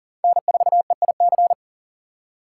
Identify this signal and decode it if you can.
Morse code, heard as a keyed tone, about 40 wpm, 700 Hz. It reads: N4EIC
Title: USA Call Signs - 5 Characters - 3x Character Spacing - 40wpm